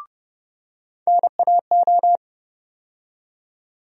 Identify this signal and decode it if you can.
Morse code, heard as a keyed tone, about 30 wpm, 700 Hz. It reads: NAO
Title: USA Call Sign Postfixes - 3 Letters 30wpm